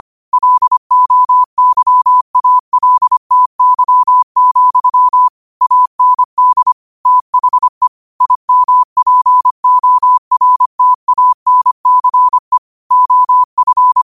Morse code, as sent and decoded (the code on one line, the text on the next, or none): .-.. --- -.-- .- .-.. - -.-- --..-- / .- -. -.. / - .... / .. -- .--. --- .-. - .- -. -.-. / --- ..-.
LOYALTY, AND THE IMPORTANCE OF